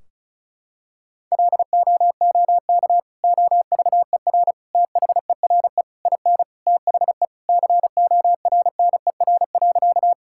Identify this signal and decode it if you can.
Morse code, heard as a keyed tone, about 34 wpm, 700 Hz. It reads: LOOK OVER THERE IN THE CORNER.
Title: Sentences from Top 1600 Words 35wpm